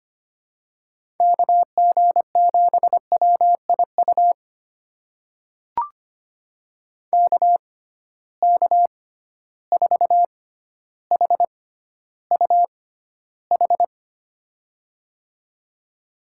Morse code, heard as a keyed tone, about 25 wpm, spaced as for 13 wpm, 700 Hz. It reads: KG7WIU KK4HUH
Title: USA Call Signs - 6 Characters - 6x Character Spacing - 25wpm